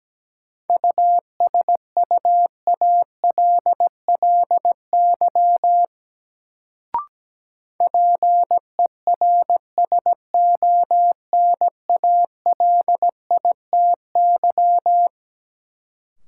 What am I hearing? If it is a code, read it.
Morse code, heard as a keyed tone, about 17 wpm, 700 Hz. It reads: USUALLY PERSONALITY
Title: Sentences from Top 1400 Words - New Word Drill Alt - 17wpm